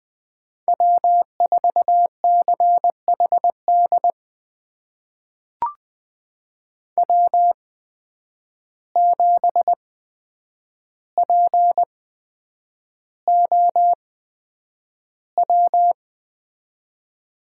Morse code, heard as a keyed tone, about 20 wpm, spaced as for 9 wpm, 700 Hz. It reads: W4CHD W7POW
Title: USA Call Signs - 5 Characters - 8x Character Spacing - 20wpm